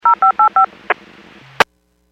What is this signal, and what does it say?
DTMF, heard as telephone touch tones, tones of about 110 ms. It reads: 0585
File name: WPS / 00647 21-41 WPS 451.25 WPS